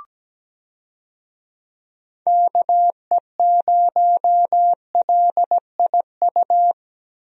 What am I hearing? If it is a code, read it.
Morse code, heard as a keyed tone, about 17 wpm, 700 Hz. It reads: KE0LIU
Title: USA Call Signs - 6 Characters - 17wpm